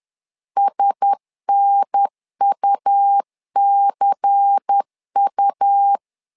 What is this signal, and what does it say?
Morse code, heard as a keyed tone, about 10 wpm, 800 Hz. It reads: SNUCU
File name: LEZIONI / 5 LBVC / 5 LBVC 10wpm 2349